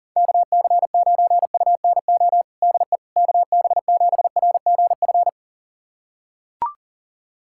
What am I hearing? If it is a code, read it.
Morse code, heard as a keyed tone, about 40 wpm, 700 Hz. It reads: KC9UNO DE KB7RGF
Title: USA Callsign DE Callsign 40wpm